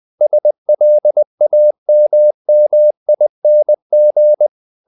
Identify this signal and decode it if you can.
Morse code, heard as a keyed tone, about 20 wpm, 600 Hz. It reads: SLAMMING
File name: the old man and the sea part 2 20wpm